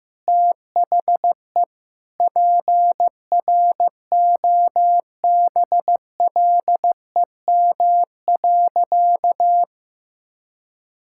Morse code, heard as a keyed tone, about 15 wpm, 700 Hz. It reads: THE PROBLEM.
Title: Sentences from Top 1000 Words - Review Part 5 - 15wpm